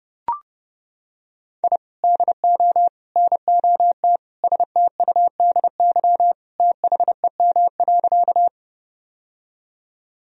Morse code, heard as a keyed tone, about 30 wpm, 700 Hz. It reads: I DO NOT STUDY THEM.
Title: Sentences from Top 1000 Words - Review Part 1 - 30wpm